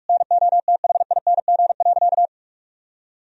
Morse code, heard as a keyed tone, about 46 wpm, 700 Hz. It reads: NOTHING.
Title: Sentences from Top 1000 Words - Review Part 2 - 45wpm